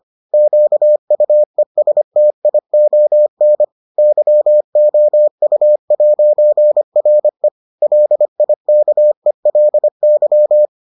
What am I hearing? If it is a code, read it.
Morse code, heard as a keyed tone, about 25 wpm, 600 Hz. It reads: QUESTION YOU'RE LIKELY